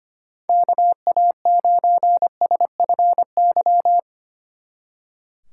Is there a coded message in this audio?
Yes: Morse code from 0.5 to 4.0 s, heard as a keyed tone, about 32 wpm, 700 Hz. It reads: KA9SFY